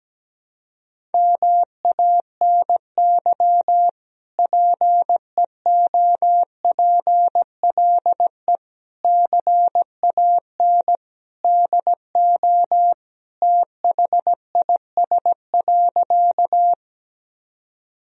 Morse code, heard as a keyed tone, about 17 wpm, 700 Hz. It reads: MANY PEOPLE CAN DO THIS.
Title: Sentences from Top 1000 Words - Review Part 1 - 17wpm